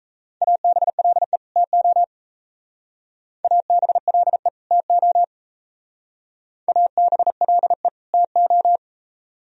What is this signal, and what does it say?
Morse code, heard as a keyed tone, about 36 wpm, 690 Hz. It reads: ABLE TO ABLE TO ABLE TO